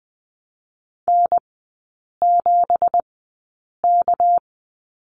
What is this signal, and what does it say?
Morse code, heard as a keyed tone, about 20 wpm, 700 Hz. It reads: N 7 K
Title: 198 Lesson 32e - Letter k - Three Characters 20wpm